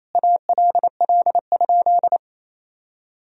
Morse code, heard as a keyed tone, about 28 wpm, 700 Hz. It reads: ALL?
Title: Sentences from Top 200 Words and Abbreviations 28wpm